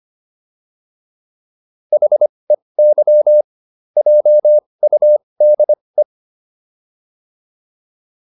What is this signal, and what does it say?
Morse code, heard as a keyed tone, about 25 wpm, spaced as for 20 wpm, 600 Hz. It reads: HEY JUDE